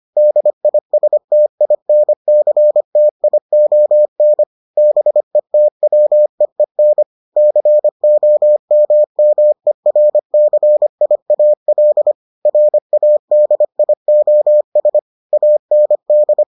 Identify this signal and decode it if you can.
Morse code, heard as a keyed tone, about 25 wpm, 600 Hz. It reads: DISTINCTION BETWEEN COMMERCIAL RADIOS AND